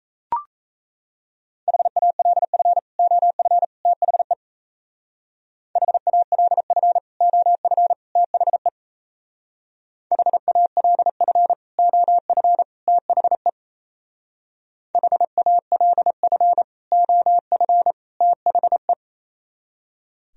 Morse code, heard as a keyed tone, about 33 wpm, 700 Hz. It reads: HALF OF THE HALF OF THE HALF OF THE HALF OF THE
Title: Speed-Racing - Sets of 3 Words 28wpm